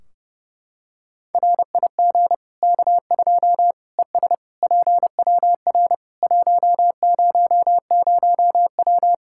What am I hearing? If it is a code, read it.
Morse code, heard as a keyed tone, about 30 wpm, 700 Hz. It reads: RIG K2 ES PWR 100W
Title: K1USN SST QSO Practice 30wpm